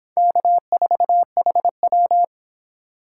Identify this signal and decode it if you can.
Morse code, heard as a keyed tone, about 25 wpm, 700 Hz. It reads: K4HW